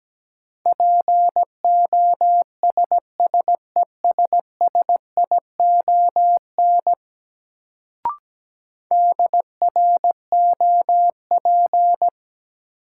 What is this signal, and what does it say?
Morse code, heard as a keyed tone, about 17 wpm, 700 Hz. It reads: POSSESSION DROP
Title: Sentences from Top 1400 Words - New Word Drill - 17wpm